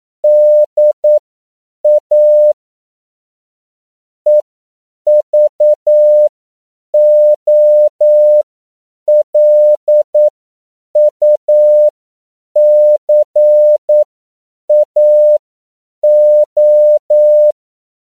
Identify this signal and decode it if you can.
Morse code, heard as a keyed tone, about 9 wpm, spaced as for 7 wpm, 600 Hz. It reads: DA EVOLUCAO